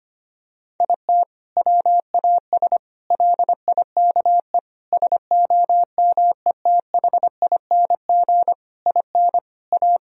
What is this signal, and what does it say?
Morse code, heard as a keyed tone, about 25 wpm, 700 Hz. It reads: IT WAS LIKE SOMETHING IN A
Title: Sentences from Top 1000 Words - Review Part 1 - 25wpm